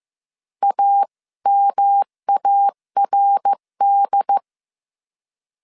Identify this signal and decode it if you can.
Morse code, heard as a keyed tone, about 15 wpm, 800 Hz. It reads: AMARD
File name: CW / DET / 1 3 1500C S A 14wpm 2447